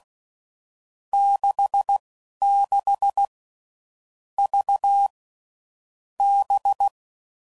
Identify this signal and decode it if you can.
Morse code, heard as a keyed tone, about 16 wpm, spaced as for 11 wpm, 800 Hz. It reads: 66 V B
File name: Groups / Basic Letter Groups / VB46